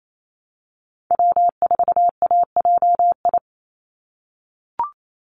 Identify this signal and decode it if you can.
Morse code, heard as a keyed tone, about 28 wpm, 700 Hz. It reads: W4AJI
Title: USA Call Signs 28wpm